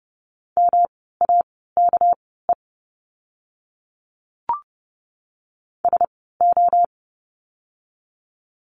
Morse code, heard as a keyed tone, about 30 wpm, spaced as for 17 wpm, 700 Hz. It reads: MAKE SO
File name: course / 30:10wpm / 230 Lesson 37f - Letter x - Words 30wpm